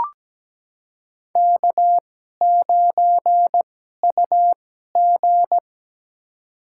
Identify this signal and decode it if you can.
Morse code, heard as a keyed tone, about 17 wpm, spaced as for 12 wpm, 700 Hz. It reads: K9UG
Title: USA Call Signs - 2x Character Spacing - 17wpm